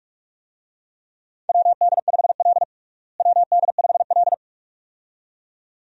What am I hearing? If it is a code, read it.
Morse code, heard as a keyed tone, about 46 wpm, 700 Hz. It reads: WD5L WD5L
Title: USA Call Signs - 4 Characters - Repeated 2x - 45wpm